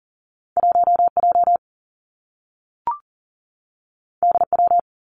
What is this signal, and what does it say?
Morse code, heard as a keyed tone, about 40 wpm, 700 Hz. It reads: 1J DW
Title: Two Challenging Letter-Number Combinations 40wpm